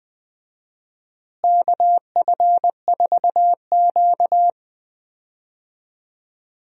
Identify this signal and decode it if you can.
Morse code, heard as a keyed tone, about 20 wpm, 700 Hz. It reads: KF4Q
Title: USA Call Signs - 4 Characters - Repeated 2x - 20wpm